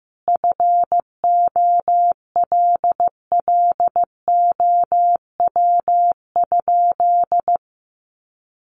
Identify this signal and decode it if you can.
Morse code, heard as a keyed tone, about 15 wpm, 700 Hz. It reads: FOLLOW?
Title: Sentences from Top 1100 Words 15wpm